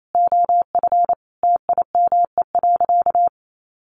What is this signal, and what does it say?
Morse code, heard as a keyed tone, about 28 wpm, 700 Hz. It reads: OF TIME.
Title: Sentences from Top 1500 Words 28wpm